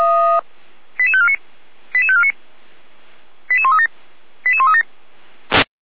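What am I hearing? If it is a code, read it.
Five-tone selective call (ZVEI), heard as quick runs of five tones, tones of about 70 ms. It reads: 88549 88549 88137 88137
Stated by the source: ZVEI five-tone selective call 88402, received by SDR on the feuerwehr-orlamuende fire brigade audio server.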